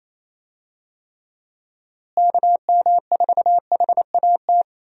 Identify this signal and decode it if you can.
Morse code, heard as a keyed tone, about 28 wpm, 700 Hz. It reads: KM4HAT
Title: USA Call Signs - 6 Characters - Repeated 2x - 28wpm